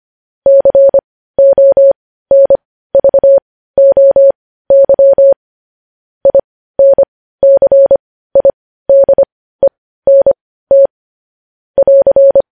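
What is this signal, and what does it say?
Morse code, heard as a keyed tone, about 25 wpm, spaced as for 15 wpm, 550 Hz. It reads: CONVOY INCIDENT <AR>
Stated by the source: Morse code transcription of news headlines.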